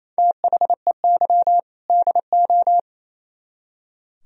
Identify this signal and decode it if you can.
Morse code, heard as a keyed tone, about 36 wpm, 690 Hz. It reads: THEY DO